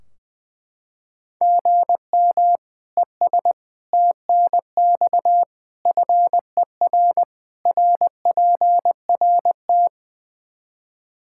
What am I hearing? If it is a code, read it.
Morse code, heard as a keyed tone, about 21 wpm, 700 Hz. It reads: GM ES TNX FER RPRT
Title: Practice QSO - Exchange 1 - Beginning - 20wpm